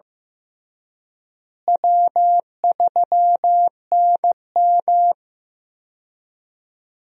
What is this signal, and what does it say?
Morse code, heard as a keyed tone, about 15 wpm, 700 Hz. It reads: W3NM
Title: USA Call Signs - 4 Characters - Repeated 3x - 15wpm